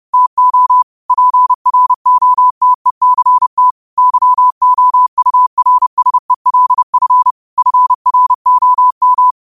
Morse code, heard as a keyed tone, about 30 wpm, 1000 Hz. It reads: TO PROTECT YOURSELF FROM
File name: qotdfiles / 30wpm